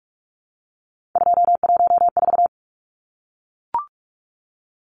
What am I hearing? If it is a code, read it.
Morse code, heard as a keyed tone, about 44 wpm, 700 Hz. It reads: 214
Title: Numbers - 3-Digit 45wpm